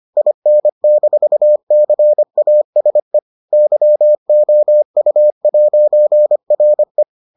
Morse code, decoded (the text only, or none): IN-CASE YOU'RE